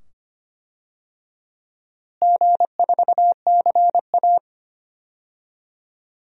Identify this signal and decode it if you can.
Morse code, heard as a keyed tone, about 25 wpm, 700 Hz. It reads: G4CA
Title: DXCC-100 Call Signs - 4 Characters - Repeated 2x - 25wpm